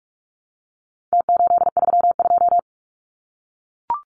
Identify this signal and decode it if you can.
Morse code, heard as a keyed tone, about 44 wpm, 700 Hz. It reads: T832
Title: Numbers - Contest Style - 4-Digit 45wpm